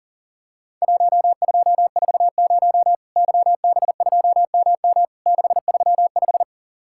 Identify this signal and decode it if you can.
Morse code, heard as a keyed tone, about 40 wpm, 700 Hz. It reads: 1240 YB2MM 635
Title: Worked All Europe DX Contest - QTC Exchange - 40wpm